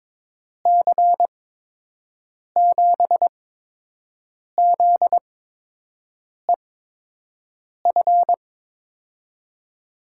Morse code, heard as a keyed tone, about 22 wpm, spaced as for 6 wpm, 700 Hz. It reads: C7ZEF
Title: USA Call Signs - 8x Character Spacing - 22wpm